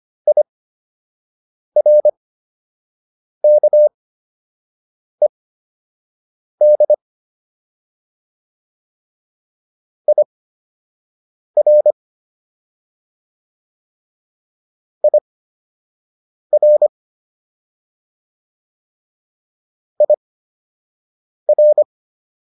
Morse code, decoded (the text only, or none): IRKED IR IR IR